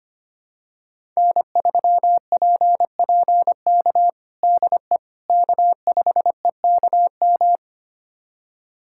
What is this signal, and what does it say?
Morse code, heard as a keyed tone, about 25 wpm, 700 Hz. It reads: N3PPK DE K5EKM
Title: USA Callsign DE Callsign - 5 Characters - 25wpm